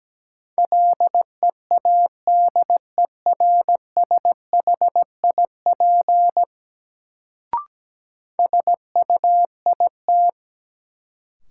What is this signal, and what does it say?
Morse code, heard as a keyed tone, about 17 wpm, 700 Hz. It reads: LEADERSHIP SUIT
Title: Sentences from Top 1600 Words - New Word Drill Alt - 17wpm